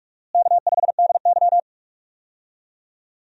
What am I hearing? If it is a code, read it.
Morse code, heard as a keyed tone, about 46 wpm, 700 Hz. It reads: K5DY
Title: USA Call Signs - 4 Characters - Repeated 3x - 45wpm